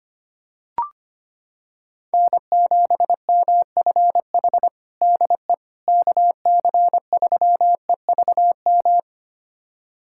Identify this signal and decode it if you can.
Morse code, heard as a keyed tone, about 25 wpm, 700 Hz. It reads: N7MFH DE KC3EVM
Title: USA Callsign DE Callsign 25wpm